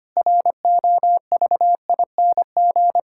Morse code, heard as a keyed tone, about 25 wpm, 700 Hz. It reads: ROVING
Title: Pangram Challenge v2 - 25wpm